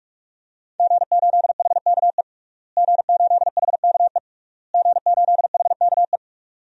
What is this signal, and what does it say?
Morse code, heard as a keyed tone, about 46 wpm, 700 Hz. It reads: G8HKE G8HKE G8HKE
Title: DXCC-100 Call Signs - 5 Characters - Repeated 3x - 45wpm